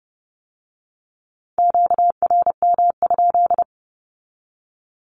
Morse code, heard as a keyed tone, about 30 wpm, 700 Hz. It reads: QRM?
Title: Q-Codes with Spoken Meaning - 30wpm